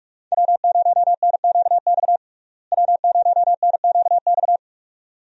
Morse code, heard as a keyed tone, about 44 wpm, 700 Hz. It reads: W0NQX W0NQX
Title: USA Call Signs - 5 Characters - Repeated 2x - 45wpm